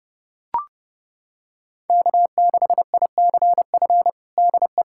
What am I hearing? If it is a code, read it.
Morse code, heard as a keyed tone, about 30 wpm, 700 Hz. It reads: K6ICF DE
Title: USA Callsign DE Callsign - 5 Characters - 30wpm